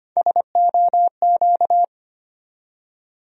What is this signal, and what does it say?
Morse code, heard as a keyed tone, about 25 wpm, 700 Hz. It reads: SOQ
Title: USA Call Sign Postfixes - 3 Letters 25wpm